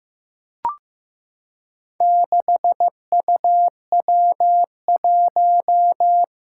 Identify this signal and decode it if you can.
Morse code, heard as a keyed tone, about 15 wpm, 700 Hz. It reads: 6UW1
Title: Four Challenging Letter-Number Combinations 15wpm